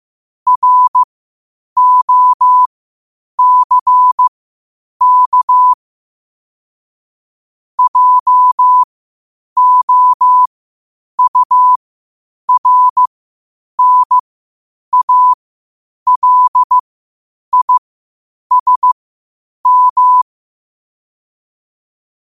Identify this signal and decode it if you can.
Morse code, heard as a keyed tone, about 15 wpm, spaced as for 8 wpm, 1000 Hz. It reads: ROCK JOURNALISM
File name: qotdfiles / 8wpm